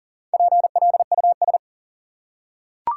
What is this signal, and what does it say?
Morse code, heard as a keyed tone, about 40 wpm, 700 Hz. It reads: PLUS
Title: Four Letter Words - 40wpm